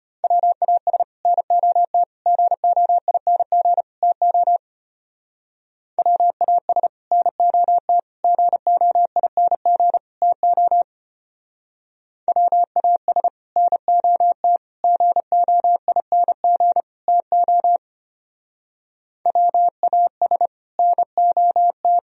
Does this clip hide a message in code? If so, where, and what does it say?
Yes: Morse code from 0.2 to 22.0 s, heard as a keyed tone, about 32 wpm, 700 Hz. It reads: WAS NOT GOING TO WAS NOT GOING TO WAS NOT GOING TO WAS NOT